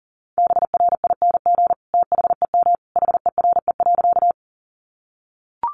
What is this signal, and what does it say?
Morse code, heard as a keyed tone, about 40 wpm, 700 Hz. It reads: BRING THEM HERE.
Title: Sentences from Top 500 Words 40wpm